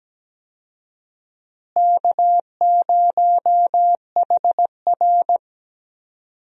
Morse code, heard as a keyed tone, about 17 wpm, 700 Hz. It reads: K0HR
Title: USA Call Signs - 4 Characters - Repeated 3x - 17wpm